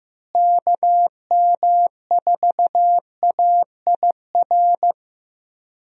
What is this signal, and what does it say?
Morse code, heard as a keyed tone, about 15 wpm, 700 Hz. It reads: KM4AIR
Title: USA Call Signs - 6 Characters - Repeated 2x - 15wpm